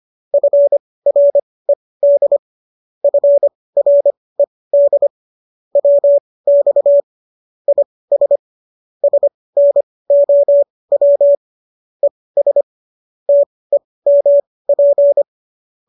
Morse code, heard as a keyed tone, about 25 wpm, spaced as for 18 wpm, 570 Hz. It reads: FRED FRED WX IS SNOW ES TEMP